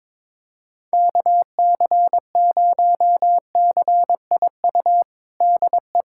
Morse code, heard as a keyed tone, about 21 wpm, 700 Hz. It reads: KC0CIU DE
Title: USA Callsign DE Callsign - 6 Characters - 22wpm